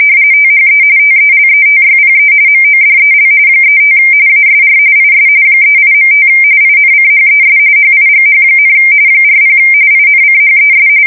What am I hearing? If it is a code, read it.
RTTY, about 45.45 baud. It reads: THE TELETYPE MODEL 35 ASR IS 38.5 INCHES HIGH, 40 INCHES
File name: fsk170 / teletype-corp